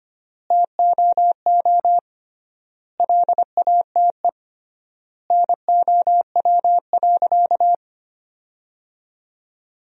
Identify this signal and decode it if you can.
Morse code, heard as a keyed tone, about 25 wpm, 700 Hz. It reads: TOO LATE NOW.
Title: 3x Word-Spacing - Sentences from Top 300 Words - 25wpm